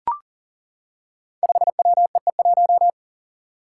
Morse code, heard as a keyed tone, about 40 wpm, 700 Hz. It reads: 5WEE1
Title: Five Challenging Letter-Number Combinations 40wpm